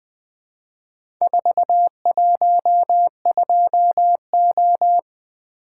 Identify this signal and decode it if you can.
Morse code, heard as a keyed tone, about 20 wpm, 700 Hz. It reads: 412O
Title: Four Letter-Number Combinations - 20wpm